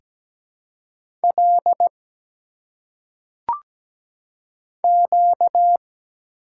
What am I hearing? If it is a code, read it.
Morse code, heard as a keyed tone, about 17 wpm, 700 Hz. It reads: L Q
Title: Single Letters - Phonetic Pronunciation 17wpm